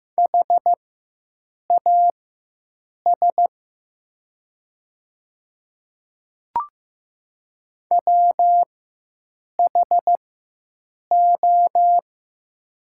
Morse code, heard as a keyed tone, about 15 wpm, spaced as for 7 wpm, 700 Hz. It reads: HAS WHO
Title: 4x Character Spacing - Top 200 Words - 15wpm